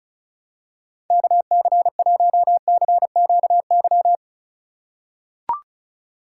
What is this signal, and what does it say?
Morse code, heard as a keyed tone, about 34 wpm, 700 Hz. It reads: KC1CQY
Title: USA Call Signs - 6 Characters - 2x Character Spacing - 35wpm